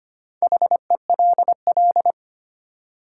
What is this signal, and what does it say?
Morse code, heard as a keyed tone, about 25 wpm, 700 Hz. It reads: HELL
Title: Sets of 5 Words Encore - New Word Drill - 25wpm